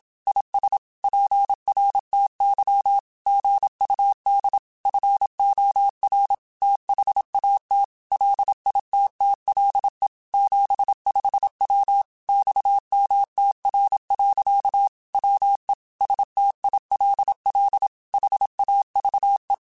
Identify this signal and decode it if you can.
Morse code, heard as a keyed tone, about 27 wpm, 800 Hz. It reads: IS PRTY GUD FOR THAT LITTLE 75W XMTR. WE STILL HAVE